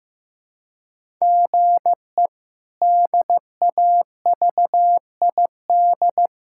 Morse code, heard as a keyed tone, about 15 wpm, 700 Hz. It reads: GE DAVID
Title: K1USN SST QSO Practice - 15wpm